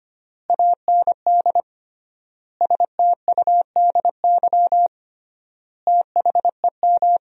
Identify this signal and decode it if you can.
Morse code, heard as a keyed tone, about 25 wpm, 700 Hz. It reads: AND STUDY THEM
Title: 3x Word-Spacing - Sentences from Top 300 Words - 25wpm